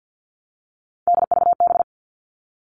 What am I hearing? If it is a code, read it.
Morse code, heard as a keyed tone, about 50 wpm, 700 Hz. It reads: DVB
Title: All Cut Numbers - 3-Digit 50wpm